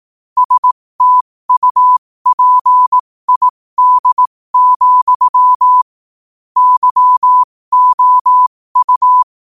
Morse code, heard as a keyed tone, about 18 wpm, spaced as for 15 wpm, 1000 Hz. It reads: STUPID, YOU